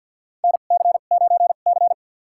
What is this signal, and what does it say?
Morse code, heard as a keyed tone, about 52 wpm, spaced as for 36 wpm, 700 Hz. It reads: NC9C
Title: USA Call Signs - 4 Characters - 2x Character Spacing - 50wpm